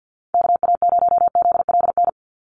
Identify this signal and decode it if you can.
Morse code, heard as a keyed tone, about 50 wpm, 700 Hz. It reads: KA0ZLN
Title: USA Call Signs 50wpm